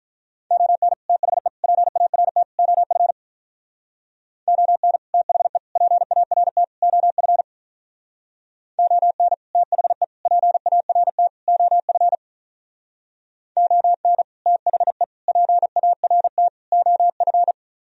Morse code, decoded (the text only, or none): ON THE PART OF ON THE PART OF ON THE PART OF ON THE PART OF